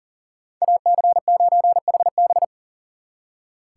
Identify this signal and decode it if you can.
Morse code, heard as a keyed tone, about 50 wpm, 700 Hz. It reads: AC9HB
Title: Speed Racing - USA 2x2 Call Signs 40wpm